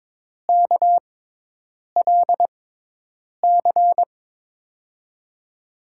Morse code, heard as a keyed tone, about 22 wpm, spaced as for 8 wpm, 700 Hz. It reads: KLC
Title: USA Call Signs - 6 Characters - 6x Character Spacing - 22wpm